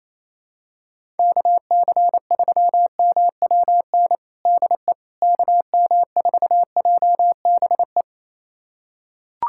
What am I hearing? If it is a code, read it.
Morse code, heard as a keyed tone, about 28 wpm, 700 Hz. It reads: KC3MWN DE KM4JBE
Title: USA Callsign DE Callsign - 6 Characters - 28wpm